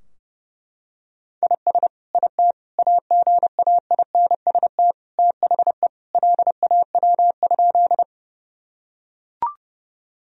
Morse code, heard as a keyed tone, about 30 wpm, 700 Hz. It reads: IS IT AGAINST THE LAW?
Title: Sentences from Top 1000 Words - Review Part 3 - 30wpm